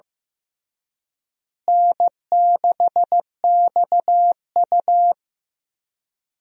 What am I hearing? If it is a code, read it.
Morse code, heard as a keyed tone, about 15 wpm, 700 Hz. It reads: N6XU
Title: USA Call Signs - 4 Characters - Repeated 3x - 15wpm